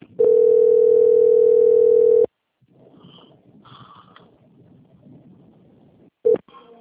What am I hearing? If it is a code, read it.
Ringback tone (North America), 440 + 480 Hz.